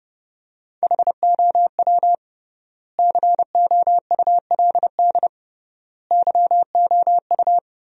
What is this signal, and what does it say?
Morse code, heard as a keyed tone, about 30 wpm, 700 Hz. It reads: HOW COULD YOU
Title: 3x Word-Spacing - Sentences from Top 300 Words - 30wpm